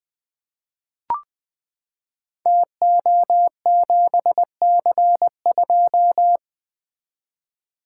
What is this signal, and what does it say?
Morse code, heard as a keyed tone, about 20 wpm, 700 Hz. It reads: TO7C2
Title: Code Groups - 5 Letter:Numbers - 20wpm